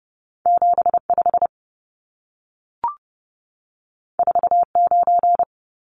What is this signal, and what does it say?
Morse code, heard as a keyed tone, about 30 wpm, 700 Hz. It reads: 75 49
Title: Numbers - 2-Digit 30wpm